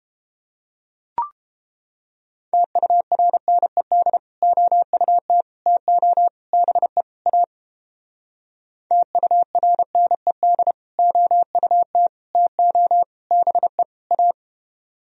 Morse code, heard as a keyed tone, about 29 wpm, 700 Hz. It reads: TURNED OUT TO BE A TURNED OUT TO BE A